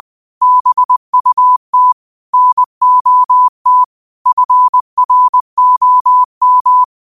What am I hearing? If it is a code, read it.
Morse code, heard as a keyed tone, about 20 wpm, 1000 Hz. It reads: BUT NOT FROM
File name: qotdfiles / 20wpm